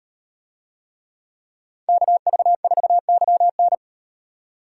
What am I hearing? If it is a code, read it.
Morse code, heard as a keyed tone, about 36 wpm, 700 Hz. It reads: KV4YN